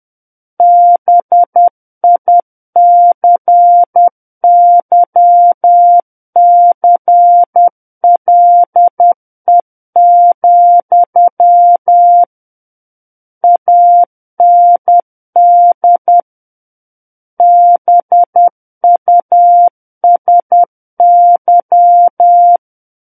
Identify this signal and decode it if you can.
Morse code, heard as a keyed tone, about 10 wpm, 700 Hz. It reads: BICYCLE, AND BUSY
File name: Book One / 01-Chapter 1 - 10 WPM